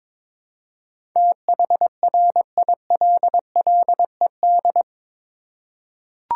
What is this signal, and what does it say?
Morse code, heard as a keyed tone, about 22 wpm, 700 Hz. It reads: THRILLED
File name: practice / 22wpm / Eight Letter Words - 22wpm